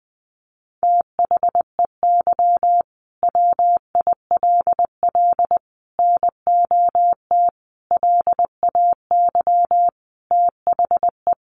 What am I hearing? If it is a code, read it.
Morse code, heard as a keyed tone, about 20 wpm, 700 Hz. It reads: THEY WILL NOT LAY THE
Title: Sentences from Top 1500 Words 20wpm